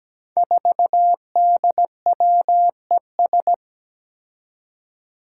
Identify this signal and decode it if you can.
Morse code, heard as a keyed tone, about 17 wpm, 700 Hz. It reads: 4DWES
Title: Five Challenging Letter-Number Combinations 17wpm